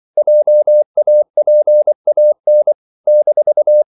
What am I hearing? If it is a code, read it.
Morse code, heard as a keyed tone, about 24 wpm, 600 Hz. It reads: JAPAN -
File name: CW / Practice / NationalCapitals24